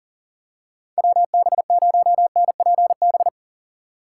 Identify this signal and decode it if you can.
Morse code, heard as a keyed tone, about 40 wpm, 700 Hz. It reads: WB0NPB
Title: USA Call Signs - 6 Characters - 4x Character Spacing - 40wpm